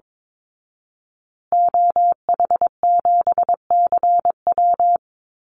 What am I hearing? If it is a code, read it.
Morse code, heard as a keyed tone, about 22 wpm, 700 Hz. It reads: OH7CW
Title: Call Signs Repeated 2x - 22wpm